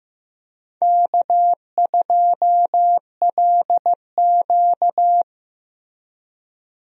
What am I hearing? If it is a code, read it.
Morse code, heard as a keyed tone, about 15 wpm, 700 Hz. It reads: K2LQ
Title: USA Call Signs - 4 Characters - Repeated 3x - 15wpm